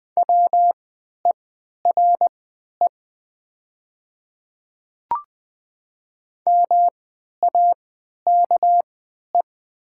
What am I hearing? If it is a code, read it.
Morse code, heard as a keyed tone, about 20 wpm, spaced as for 11 wpm, 700 Hz. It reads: WERE MAKE
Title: 3x Character Spacing - Top 100 Words - 20wpm